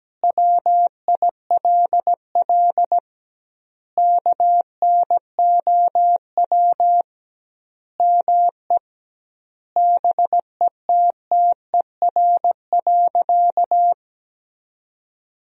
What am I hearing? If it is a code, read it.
Morse code, heard as a keyed tone, about 17 wpm, 700 Hz. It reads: WILL KNOW ME BETTER.
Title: 2x Word-Spacing - Sentences from Top 400 Words - 17wpm